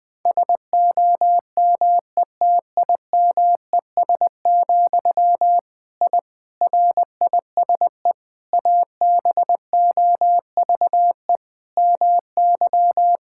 Morse code, heard as a keyed tone, about 21 wpm, 700 Hz. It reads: SOMETIMES, I RISE ABOVE MY
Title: Sentences from Top 1000 Words - Review Part 6 - 20wpm